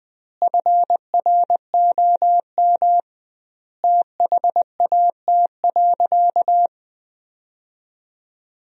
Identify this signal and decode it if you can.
Morse code, heard as a keyed tone, about 20 wpm, 700 Hz. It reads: FROM THAT.
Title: 2x Word-Spacing - Sentences from Top 300 Words - 20wpm